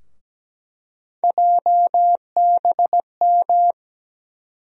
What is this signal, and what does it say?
Morse code, heard as a keyed tone, about 17 wpm, 700 Hz. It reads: JBM
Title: USA Call Sign Postfixes - 3 Letters 17wpm